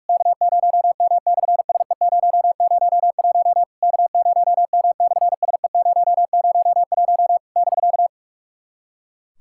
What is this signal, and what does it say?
Morse code, heard as a keyed tone, about 44 wpm, 700 Hz. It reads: K0M/SE001 K0M/SE001 <BK>
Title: SOTA QSO Practice 45wpm